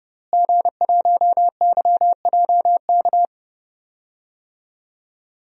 Morse code, heard as a keyed tone, about 30 wpm, 700 Hz. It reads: G1YJK
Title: DXCC-100 Call Signs - 5 Characters - Repeated 2x - 30wpm